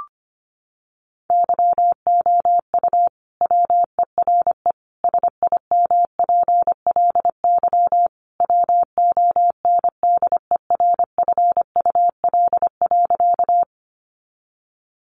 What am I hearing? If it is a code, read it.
Morse code, heard as a keyed tone, about 25 wpm, 700 Hz. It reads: YOU WERE SIMPLY WONDERFUL.
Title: Sentences from Top 1100 Words 25wpm